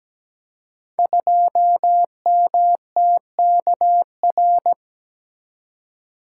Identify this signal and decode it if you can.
Morse code, heard as a keyed tone, about 17 wpm, 700 Hz. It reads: 2MTKR
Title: Code Groups - 5 Letter:Numbers - 17wpm